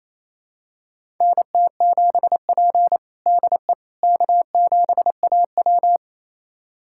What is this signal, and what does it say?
Morse code, heard as a keyed tone, about 28 wpm, 700 Hz. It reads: NT7P DE K7AW
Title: USA Callsign DE Callsign - 4 Characters - 28wpm